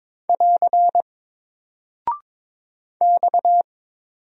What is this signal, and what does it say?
Morse code, heard as a keyed tone, about 22 wpm, 700 Hz. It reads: <AR> X
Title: Single Character (Letter:Number:Punctuation:Prosign) Alt 22wpm